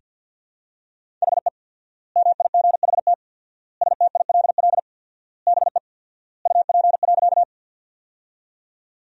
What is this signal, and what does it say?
Morse code, heard as a keyed tone, about 50 wpm, 700 Hz. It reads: HE MIGHT STILL BE UP.
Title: 4x Word-Spacing - Sentences from Top 300 Words - 50wpm